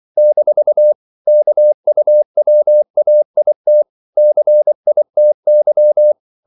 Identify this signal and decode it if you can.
Morse code, heard as a keyed tone, about 24 wpm, 600 Hz. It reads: - KUWAIT CITY
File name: CW / Practice / NationalCapitals24